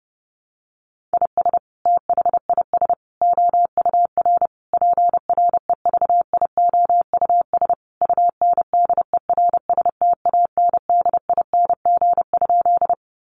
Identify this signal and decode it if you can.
Morse code, heard as a keyed tone, about 28 wpm, 700 Hz. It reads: IS THIS OUR PREVIOUS UNDERSTANDING?
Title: Sentences from Top 1500 Words 30wpm